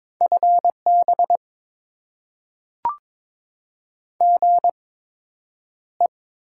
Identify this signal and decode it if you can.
Morse code, heard as a keyed tone, about 21 wpm, spaced as for 9 wpm, 700 Hz. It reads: FB GE
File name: Top 20 QSO Elements - 8x Character Spacing - 22wpm